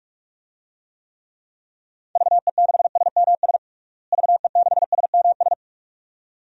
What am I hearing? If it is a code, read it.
Morse code, heard as a keyed tone, about 46 wpm, 700 Hz. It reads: VE6SMS VE6SMS
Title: DXCC-100 Call Signs - 6 Characters - Repeated 2x - 45wpm